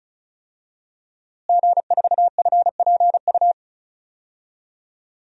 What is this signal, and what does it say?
Morse code, heard as a keyed tone, about 36 wpm, 700 Hz. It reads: G4FPU
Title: DXCC-100 Call Signs - 5 Characters - Repeated 2x - 35wpm